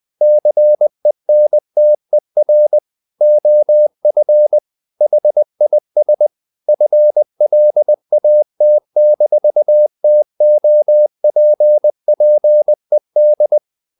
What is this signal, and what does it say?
Morse code, heard as a keyed tone, about 20 wpm, 600 Hz. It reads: CENTER OF HIS FLAT-TOPPED
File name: the old man and the sea part 2 20wpm